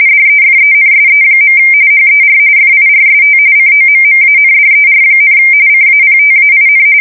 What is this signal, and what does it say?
RTTY, about 45.45 baud. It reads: DEFENSE. HER FIRST COMMANDER, CAPTAIN
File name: fsk170 / USS-New-Jersey